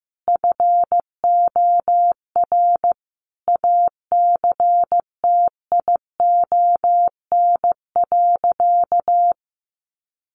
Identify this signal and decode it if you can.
Morse code, heard as a keyed tone, about 15 wpm, 700 Hz. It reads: FOR ACTION.
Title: Sentences from Top 1300 Words 15wpm